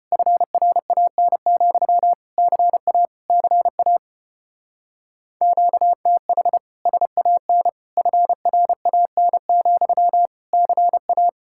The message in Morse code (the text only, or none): FRAN, CA CA QTH SAN FRAN, CA